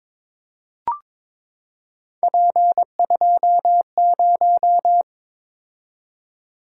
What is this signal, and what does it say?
Morse code, heard as a keyed tone, about 22 wpm, 700 Hz. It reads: P20
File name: practice / 22wpm / Three Letter-Number Combinations - 22wpm